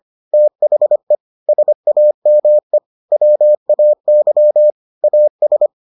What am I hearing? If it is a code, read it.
Morse code, heard as a keyed tone, about 25 wpm, 600 Hz. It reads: THE SAME WAY AS